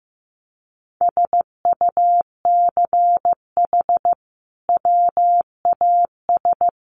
Morse code, heard as a keyed tone, about 15 wpm, 700 Hz. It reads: SUCH WAS
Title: Sentences from Top 1400 Words 15wpm